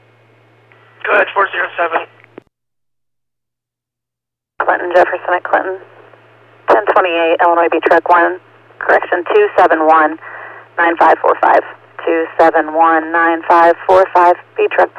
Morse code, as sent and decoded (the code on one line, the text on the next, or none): none